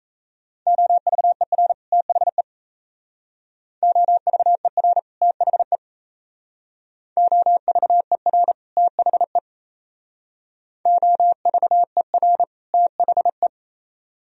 Morse code, heard as a keyed tone, about 36 wpm, 700 Hz. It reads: OVER THE OVER THE OVER THE OVER THE